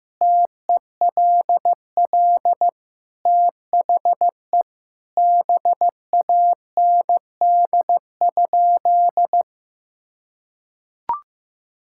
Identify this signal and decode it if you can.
Morse code, heard as a keyed tone, about 16 wpm, 700 Hz. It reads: TELL THE BAND?
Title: Sentences from Top 1600 Words 15wpm